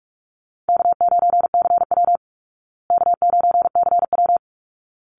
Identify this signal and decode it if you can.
Morse code, heard as a keyed tone, about 44 wpm, 700 Hz. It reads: K9CW K9CW
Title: Easy Call Signs Repeated 2x - 45wpm